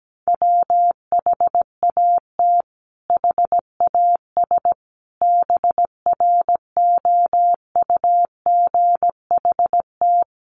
Morse code, heard as a keyed tone, about 17 wpm, 700 Hz. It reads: WHAT HAS BROUGHT
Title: Sentences from Top 1500 Words 17wpm